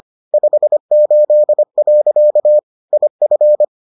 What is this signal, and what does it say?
Morse code, heard as a keyed tone, about 25 wpm, 600 Hz. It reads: 58. IF